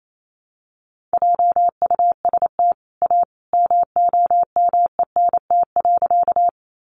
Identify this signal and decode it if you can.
Morse code, heard as a keyed tone, about 28 wpm, 700 Hz. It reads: JUST A MOMENT.
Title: Sentences from Top 1000 Words 28wpm